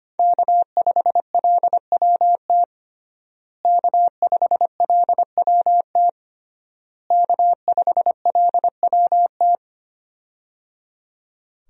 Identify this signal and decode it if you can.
Morse code, heard as a keyed tone, about 25 wpm, 700 Hz. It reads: K5LWT K5LWT K5LWT
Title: USA Call Signs - 5 Characters - Repeated 3x - 25wpm